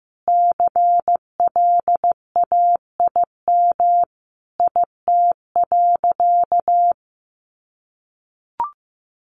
Morse code, heard as a keyed tone, about 15 wpm, 700 Hz. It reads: CLAIM IT.
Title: Sentences from Top 1500 Words 15wpm